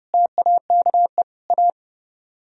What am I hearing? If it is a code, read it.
Morse code, heard as a keyed tone, about 37 wpm, 700 Hz. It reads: TAKE A
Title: Speed-Racing - Sets of 4 Words 30wpm